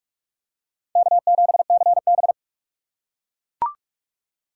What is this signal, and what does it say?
Morse code, heard as a keyed tone, about 42 wpm, 700 Hz. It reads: K7CB